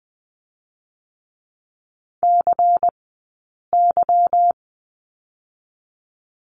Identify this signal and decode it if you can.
Morse code, heard as a keyed tone, about 20 wpm, spaced as for 8 wpm, 700 Hz. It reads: CY
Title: 144 Lesson 24d - Letter y - Two Characters 20wpm